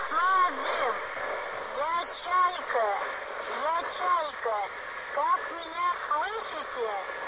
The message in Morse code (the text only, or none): none